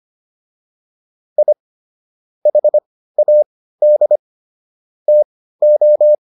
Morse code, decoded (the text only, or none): I HAD TO